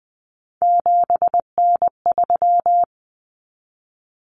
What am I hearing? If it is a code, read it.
Morse code, heard as a keyed tone, about 20 wpm, 700 Hz. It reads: 7N3
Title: Common Cut Numbers - 3-Digit 20wpm